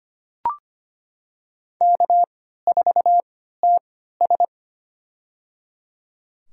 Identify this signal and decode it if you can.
Morse code, heard as a keyed tone, about 25 wpm, spaced as for 14 wpm, 700 Hz. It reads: K4TS
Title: USA Call Signs - 4 Characters - 3x Character Spacing - 25wpm